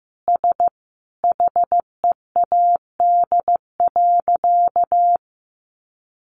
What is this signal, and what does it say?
Morse code, heard as a keyed tone, about 15 wpm, 700 Hz. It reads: S HEAD.
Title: Easy Phrases 15wpm